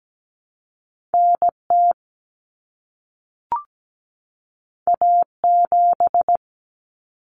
Morse code, heard as a keyed tone, about 17 wpm, 700 Hz. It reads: NT A7
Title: Common Cut Numbers - 2-Digit 17wpm